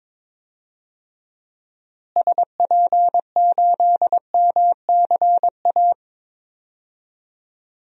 Morse code, heard as a keyed tone, about 21 wpm, 700 Hz. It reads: SP8MCA
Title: DXCC-100 Call Signs - 6 Characters - Repeated 3x - 22wpm